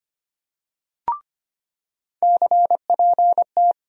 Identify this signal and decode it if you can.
Morse code, heard as a keyed tone, about 25 wpm, 700 Hz. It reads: CPT
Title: USA Call Sign Postfixes - 3 Letters 25wpm